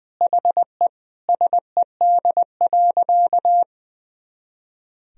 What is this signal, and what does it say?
Morse code, heard as a keyed tone, about 20 wpm, 700 Hz. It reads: HE SED.
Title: Sentences from Top 300 Words and Abbreviations 20wpm